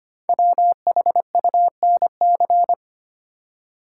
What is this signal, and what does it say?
Morse code, heard as a keyed tone, about 25 wpm, 700 Hz. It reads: WHUNC
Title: Code Groups - 5 Letters - 25wpm